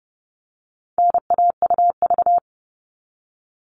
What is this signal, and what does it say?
Morse code, heard as a keyed tone, about 30 wpm, 700 Hz. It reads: NAUV
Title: All Cut Numbers - 4-Digit 30wpm